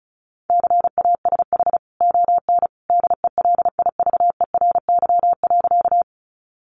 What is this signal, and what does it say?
Morse code, heard as a keyed tone, about 34 wpm, 700 Hz. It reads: CASH ON DELIVERY.
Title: Sentences from Top 1500 Words 35wpm